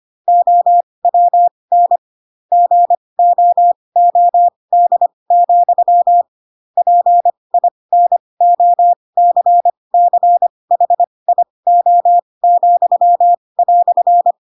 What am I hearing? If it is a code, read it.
Morse code, heard as a keyed tone, about 25 wpm, spaced as for 20 wpm, 700 Hz. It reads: OWN GOOD, PINOCCHIO,"
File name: Pinocchio / Pinocchio ch4 15 25 ik1hge